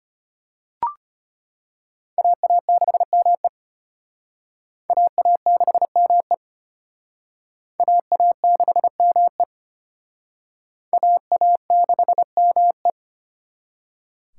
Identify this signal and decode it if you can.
Morse code, heard as a keyed tone, about 31 wpm, 700 Hz. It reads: AA6ME AA6ME AA6ME AA6ME